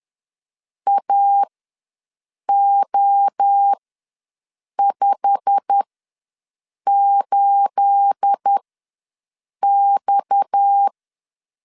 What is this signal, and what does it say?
Morse code, heard as a keyed tone, about 10 wpm, spaced as for 6 wpm, 800 Hz. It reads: AO58X
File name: LEZIONI / 9 286 ws 10wpm 4032